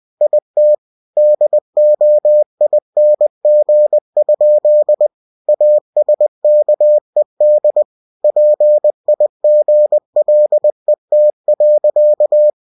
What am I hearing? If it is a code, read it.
Morse code, heard as a keyed tone, about 20 wpm, 600 Hz. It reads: IT DOING? ASKED PIGLET.
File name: cw-books / winnie-the-pooh / winnie-20wpm-page-0033